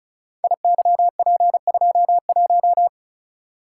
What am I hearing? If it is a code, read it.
Morse code, heard as a keyed tone, about 36 wpm, 700 Hz. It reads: IYP21
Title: Code Groups - 5 Letter:Numbers - 35wpm